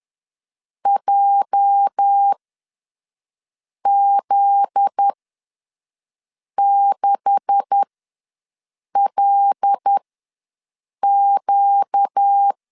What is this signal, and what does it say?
Morse code, heard as a keyed tone, about 10 wpm, spaced as for 5 wpm, 800 Hz. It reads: JZ6LQ
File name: LEZIONI / 9 286 ws 10wpm 4032